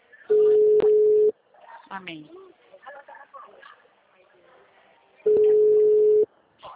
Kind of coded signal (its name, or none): telephone call progress tone